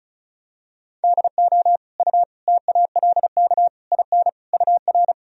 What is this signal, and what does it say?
Morse code, heard as a keyed tone, about 36 wpm, 700 Hz. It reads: DO U TALK IN UR
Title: Sentences from Top 500 Words and Abbreviations 35wpm